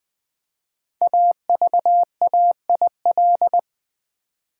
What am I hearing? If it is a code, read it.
Morse code, heard as a keyed tone, about 20 wpm, 700 Hz. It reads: AVAIL
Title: Build A Word - 5 Letters At A Time - 12-Letter Words - 20wpm